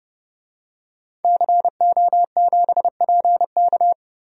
Morse code, heard as a keyed tone, about 30 wpm, 700 Hz. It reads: CO7PK
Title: DXCC-100 Call Signs - 5 Characters - Repeated 2x - 30wpm